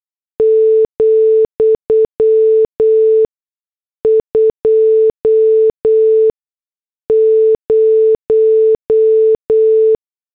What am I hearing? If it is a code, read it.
Morse code, heard as a keyed tone, about 8 wpm, spaced as for 6 wpm, 440 Hz. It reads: , 20